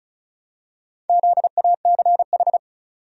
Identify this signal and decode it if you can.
Morse code, heard as a keyed tone, about 35 wpm, 700 Hz. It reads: ZACH
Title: USA Common Names - Max 4 Letters - 35wpm